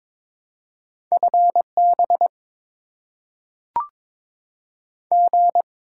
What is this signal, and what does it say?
Morse code, heard as a keyed tone, about 22 wpm, 690 Hz. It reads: FB G
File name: Top 20 QSO Elements - 6x Character Spacing - 22wpm